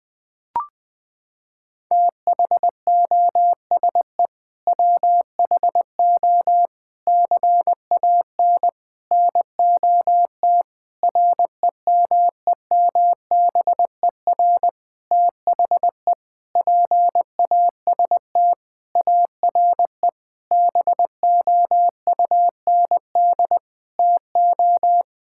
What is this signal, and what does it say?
Morse code, heard as a keyed tone, about 20 wpm, 700 Hz. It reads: THOSE WHO CAN NOT REMEMBER THE PAST ARE BOUND TO
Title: Sentences with 8-Word Phrases - 20wpm